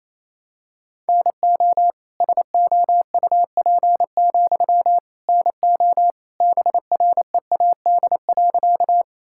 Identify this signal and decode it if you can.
Morse code, heard as a keyed tone, about 27 wpm, 700 Hz. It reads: NO SOUP, NO BREAD.
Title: Sentences from Top 1600 Words 28wpm